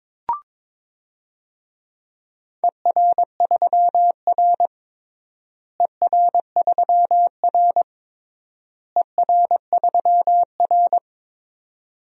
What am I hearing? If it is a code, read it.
Morse code, heard as a keyed tone, about 22 wpm, 700 Hz. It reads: ER3R ER3R ER3R
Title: DXCC-100 Call Signs - 4 Characters - Repeated 3x - 22wpm